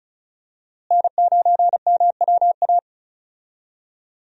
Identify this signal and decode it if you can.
Morse code, heard as a keyed tone, about 36 wpm, 700 Hz. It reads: N9MWA
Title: USA Call Signs - 5 Characters - Repeated 3x - 35wpm